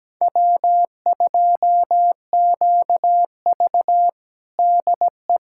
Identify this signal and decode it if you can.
Morse code, heard as a keyed tone, about 17 wpm, 700 Hz. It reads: W2QV DE
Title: USA Callsign DE Callsign 17wpm